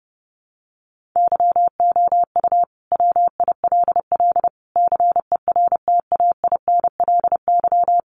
Morse code, heard as a keyed tone, about 30 wpm, 700 Hz. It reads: YOU WILL CERTAINLY
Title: Sentences from Top 1400 Words 30wpm